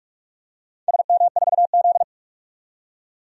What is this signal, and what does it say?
Morse code, heard as a keyed tone, about 45 wpm, 700 Hz. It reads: SM4Z
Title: DXCC-100 Call Signs - 4 Characters - 45wpm